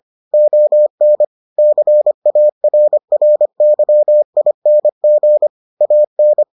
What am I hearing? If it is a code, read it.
Morse code, heard as a keyed tone, about 25 wpm, 600 Hz. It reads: ON CARRYING AN